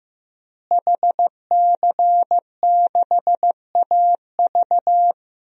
Morse code, heard as a keyed tone, about 15 wpm, 700 Hz. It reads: HC6AV
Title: DXCC-100 Call Signs - 5 Characters - Repeated 2x - 15wpm